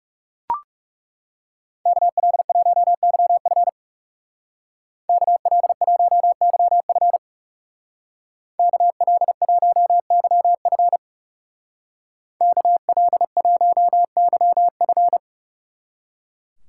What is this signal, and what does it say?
Morse code, heard as a keyed tone, about 35 wpm, 700 Hz. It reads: KL1YF KL1YF KL1YF KL1YF
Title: Speed Racing - USA Call Signs 30wpm